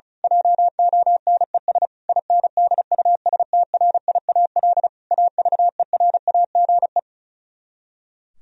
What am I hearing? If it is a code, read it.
Morse code, heard as a keyed tone, about 32 wpm, 700 Hz. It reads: JONES INDUSTRIAL AVERAGE
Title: Speed-Racing - Sets of 5 Words 35wpm